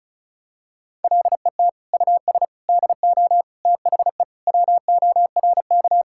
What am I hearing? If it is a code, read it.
Morse code, heard as a keyed tone, about 34 wpm, 700 Hz. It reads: LET US DO THE WORK
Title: Sentences from Top 1600 Words 35wpm